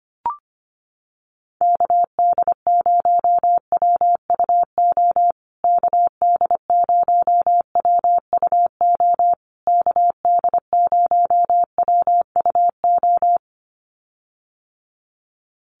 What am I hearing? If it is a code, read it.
Morse code, heard as a keyed tone, about 25 wpm, 700 Hz. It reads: KD0WUO KD0WUO KD0WUO
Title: USA Call Signs Repeated 3x 25wpm